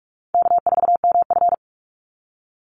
Morse code, heard as a keyed tone, about 44 wpm, 700 Hz. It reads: K4MF
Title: NA SOTA Chaser Callsigns 45wpm